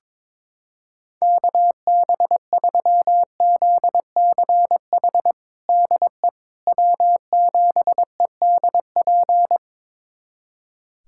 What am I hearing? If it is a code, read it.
Morse code, heard as a keyed tone, about 22 wpm, 700 Hz. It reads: KB3ZCH DE W7EDP
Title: USA Callsign DE Callsign 22wpm